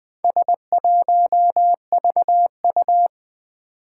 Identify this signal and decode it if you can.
Morse code, heard as a keyed tone, about 20 wpm, 700 Hz. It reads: S1VU
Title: Four Challenging Letter-Number Combinations 20wpm